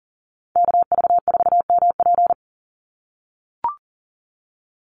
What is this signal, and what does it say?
Morse code, heard as a keyed tone, about 37 wpm, 700 Hz. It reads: KV4MP
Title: USA Call Signs 40wpm